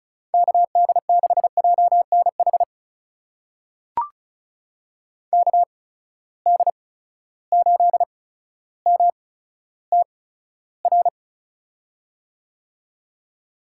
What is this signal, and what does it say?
Morse code, heard as a keyed tone, about 36 wpm, spaced as for 15 wpm, 700 Hz. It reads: KD6JNH KD8MTR
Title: USA Call Signs - 6 Characters - 8x Character Spacing - 35wpm